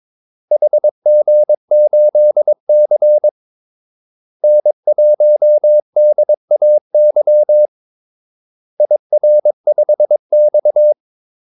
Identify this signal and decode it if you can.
Morse code, heard as a keyed tone, about 22 wpm, 600 Hz. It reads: HG8C N1DAY IR5X
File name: Callsigns-22wpm